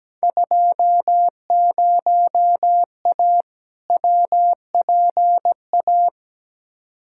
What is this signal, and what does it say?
Morse code, heard as a keyed tone, about 17 wpm, 700 Hz. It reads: 20A WPA
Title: Field Day Exchanges Encore - 17wpm